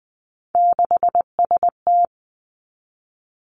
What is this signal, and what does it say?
Morse code, heard as a keyed tone, about 19 wpm, 700 Hz. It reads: 6ST
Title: Three Challenging Letter-Number Combinations 20wpm